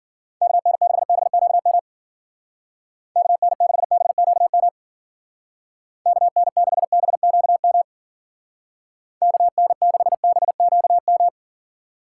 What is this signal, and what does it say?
Morse code, heard as a keyed tone, about 51 wpm, 700 Hz. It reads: KN6BQM KN6BQM KN6BQM KN6BQM